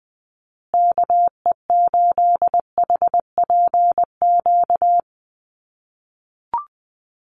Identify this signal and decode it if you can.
Morse code, heard as a keyed tone, about 20 wpm, 700 Hz. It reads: KE8HPQ
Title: USA Call Signs Repeated 3x 20wpm